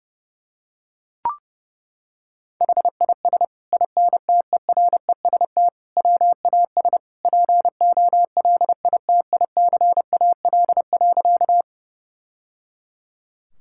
Morse code, heard as a keyed tone, about 30 wpm, 700 Hz. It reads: HIS INTEREST WAS POLITICAL.
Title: Sentences from Top 1000 Words - Review Part 5 - 30wpm